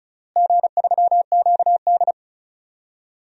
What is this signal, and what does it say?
Morse code, heard as a keyed tone, about 34 wpm, 700 Hz. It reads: G3QD
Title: DXCC-100 Call Signs - 4 Characters - Repeated 3x - 35wpm